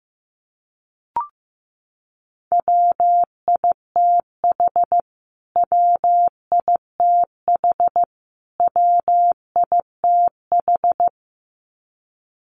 Morse code, heard as a keyed tone, about 15 wpm, 700 Hz. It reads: WITH WITH WITH
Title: Top 500 Words Repeated 15wpm